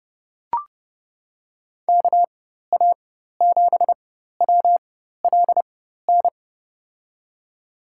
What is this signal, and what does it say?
Morse code, heard as a keyed tone, about 30 wpm, spaced as for 14 wpm, 700 Hz. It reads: KA7WLN
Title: USA Call Signs - 6 Characters - 4x Character Spacing - 30wpm